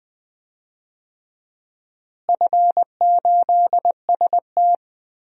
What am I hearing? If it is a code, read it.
Morse code, heard as a keyed tone, about 19 wpm, 700 Hz. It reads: F8ST
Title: DXCC-100 Call Signs - 4 Characters - Repeated 3x - 20wpm